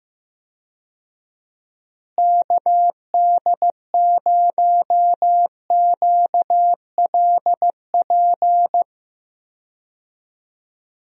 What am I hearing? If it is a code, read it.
Morse code, heard as a keyed tone, about 15 wpm, 700 Hz. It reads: KD0QLP
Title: USA Call Signs - 6 Characters - Repeated 3x - 15wpm